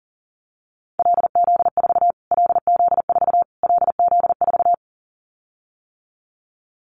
Morse code, heard as a keyed tone, about 40 wpm, 700 Hz. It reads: LZ4 LZ4 LZ4
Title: Callsign Prefixes for Top 25 DXCC - Repeated 3x - 40wpm